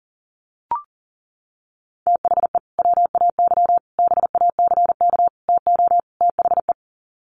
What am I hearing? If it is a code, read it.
Morse code, heard as a keyed tone, about 40 wpm, 700 Hz. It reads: THE WAY BACK TO THE
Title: Sets of 5 Words Encore2 40wpm